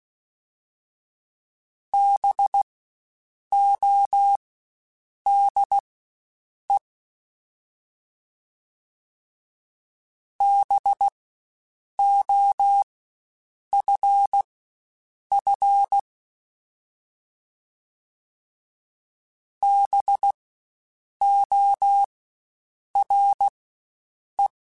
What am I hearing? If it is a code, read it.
Morse code, heard as a keyed tone, about 16 wpm, spaced as for 7 wpm, 800 Hz. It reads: BODE BOFF BORE
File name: Spaced Alphas / Spaced Alphas mp3s / Bs sp @15